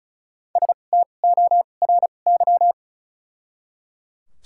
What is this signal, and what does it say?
Morse code, heard as a keyed tone, about 34 wpm, spaced as for 25 wpm, 700 Hz. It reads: STORY
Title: 2x Character Spacing - Top 300 Words - 35wpm